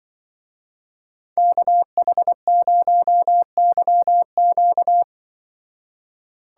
Morse code, heard as a keyed tone, about 24 wpm, 700 Hz. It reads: KH0YQ
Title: Speed Racing - USA 2x2 Call Signs 20wpm